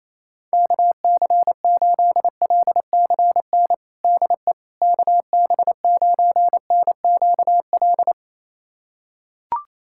Morse code, heard as a keyed tone, about 28 wpm, 700 Hz. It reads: KC8LCN DE KB9NQL
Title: USA Callsign DE Callsign - 6 Characters - 28wpm